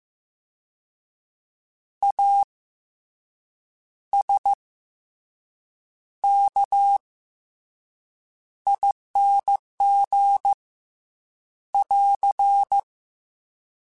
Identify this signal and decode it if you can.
Morse code, heard as a keyed tone, about 15 wpm, 800 Hz. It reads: A S K ING <AR>
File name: Special INGs / Special INGs-mp3s / SlowToFaster ing @13